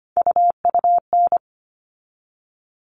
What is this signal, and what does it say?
Morse code, heard as a keyed tone, about 25 wpm, 700 Hz. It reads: UUN
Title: All Cut Numbers - 3-Digit 25wpm